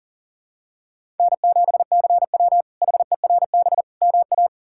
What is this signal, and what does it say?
Morse code, heard as a keyed tone, about 40 wpm, 700 Hz. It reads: N7CW HERB MA
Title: K1USN SST QSO Practice - 40wpm